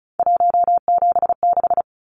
Morse code, heard as a keyed tone, about 34 wpm, 700 Hz. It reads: 176
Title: Numbers - 3-Digit 35wpm